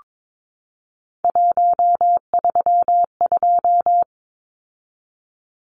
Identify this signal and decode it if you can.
Morse code, heard as a keyed tone, about 22 wpm, 700 Hz. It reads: 132
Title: Numbers - 3-Digit 22wpm